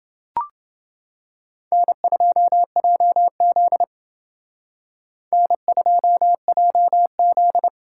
Morse code, heard as a keyed tone, about 29 wpm, 700 Hz. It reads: N2JZ N2JZ